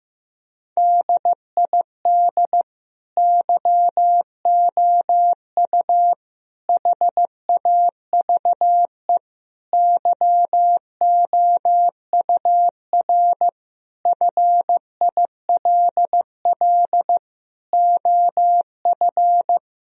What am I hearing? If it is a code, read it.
Morse code, heard as a keyed tone, about 15 wpm, 700 Hz. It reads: DID YOU HAVE YOUR FILL OF
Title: Sentences from Top 1000 Words - Review Part 3 - 15wpm